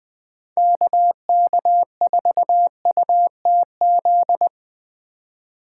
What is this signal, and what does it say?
Morse code, heard as a keyed tone, about 20 wpm, 700 Hz. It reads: KK4UTZ
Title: Speed Racing - USA 2x3 Call Signs 17wpm